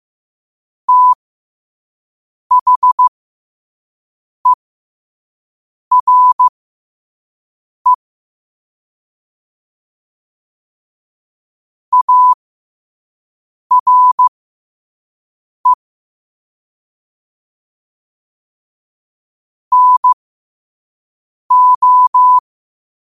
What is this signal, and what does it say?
Morse code, heard as a keyed tone, about 15 wpm, spaced as for 5 wpm, 1000 Hz. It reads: THERE ARE NO
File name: qotdfiles / 5wpm